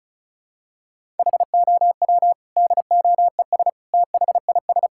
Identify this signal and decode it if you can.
Morse code, heard as a keyed tone, about 34 wpm, 700 Hz. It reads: HOW DOES THIS
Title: Sentences from Top 1000 Words - Review Part 5 - 35wpm